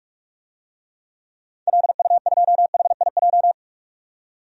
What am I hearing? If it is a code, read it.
Morse code, heard as a keyed tone, about 41 wpm, 700 Hz. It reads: LU2HIJ
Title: DXCC-100 Call Signs - 6 Characters - 45wpm